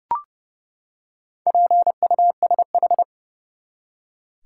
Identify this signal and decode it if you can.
Morse code, heard as a keyed tone, about 30 wpm, 700 Hz. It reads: PUSH
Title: Four Letter Words - 30wpm